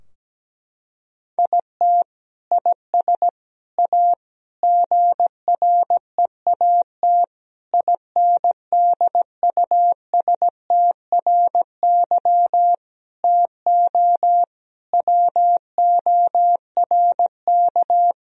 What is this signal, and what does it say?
Morse code, heard as a keyed tone, about 17 wpm, 700 Hz. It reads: IT IS A GREAT INDUSTRY TO WORK
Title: Sentences from Top 1000 Words - Review Part 4 - 17wpm